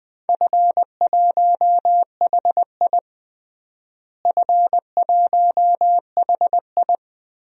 Morse code, heard as a keyed tone, about 20 wpm, 700 Hz. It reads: F1HI F1HI
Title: DXCC-100 Call Signs - 4 Characters - Repeated 3x - 20wpm